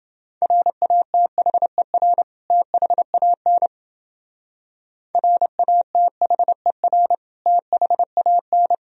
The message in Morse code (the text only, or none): RATHER THAN RATHER THAN